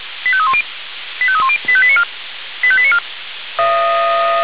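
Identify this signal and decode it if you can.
Five-tone selective call (ZVEI), heard as quick runs of five tones, tones of about 70 ms. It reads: 85310 85310 85894 85894
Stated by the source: ZVEI five-tone selective call 85320 received on SDR.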